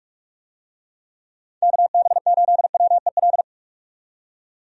Morse code, heard as a keyed tone, about 45 wpm, 700 Hz. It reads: KB8WEL
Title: USA Call Signs - 6 Characters - 45wpm